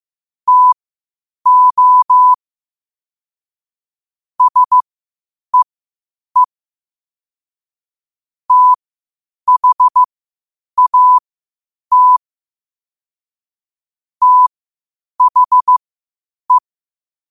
Morse code, heard as a keyed tone, about 15 wpm, spaced as for 8 wpm, 1000 Hz. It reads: TO SEE THAT THE
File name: qotdfiles / 8wpm